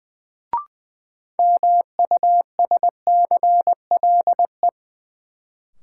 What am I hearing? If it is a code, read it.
Morse code, heard as a keyed tone, about 20 wpm, 700 Hz. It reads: MUSCLE
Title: Sentences from Top 1300 Words - New Word Drill - 20wpm